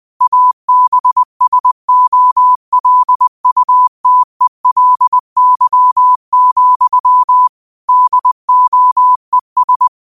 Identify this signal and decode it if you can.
Morse code, heard as a keyed tone, about 20 wpm, 1000 Hz. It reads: ABSOLUTELY, DOES